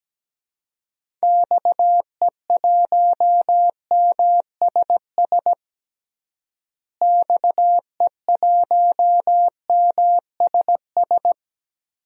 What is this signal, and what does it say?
Morse code, heard as a keyed tone, about 17 wpm, 700 Hz. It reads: XE1MSS XE1MSS
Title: DXCC-100 Call Signs - 6 Characters - Repeated 3x - 17wpm